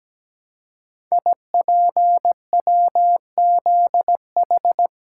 Morse code, heard as a keyed tone, about 17 wpm, 700 Hz. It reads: IPWZH
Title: Code Groups - 5 Letters - 17wpm